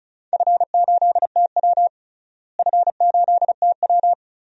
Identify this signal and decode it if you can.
Morse code, heard as a keyed tone, about 34 wpm, 700 Hz. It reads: F8TW F8TW
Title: DXCC-100 Call Signs - 4 Characters - Repeated 2x - 35wpm